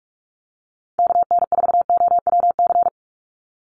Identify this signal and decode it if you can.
Morse code, heard as a keyed tone, about 46 wpm, 690 Hz. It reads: KN4OWC
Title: USA Call Signs 45wpm